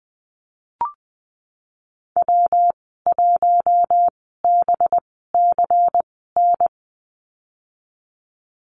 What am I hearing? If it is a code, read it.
Morse code, heard as a keyed tone, about 20 wpm, spaced as for 15 wpm, 700 Hz. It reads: W1BCN
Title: 186 Lesson 30g - Letter b - Call Signs 20wpm